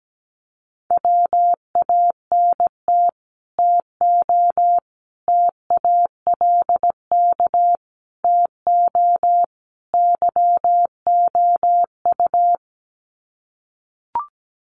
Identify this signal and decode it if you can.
Morse code, heard as a keyed tone, about 17 wpm, 700 Hz. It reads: WANT TO TALK TO YOU
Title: Sets of 5 Words Encore 17wpm